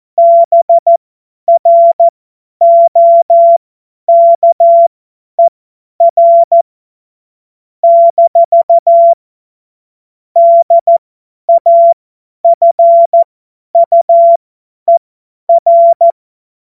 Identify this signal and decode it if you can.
Morse code, heard as a keyed tone, about 14 wpm, spaced as for 10 wpm, 680 Hz. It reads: BROKER - DAFUER
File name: Witz-01-Juli-2025-50-F70